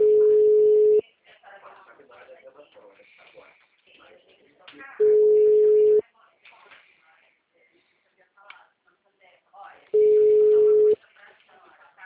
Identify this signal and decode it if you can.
Ringback tone (Europe, 425 Hz), with speech.
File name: out- -2024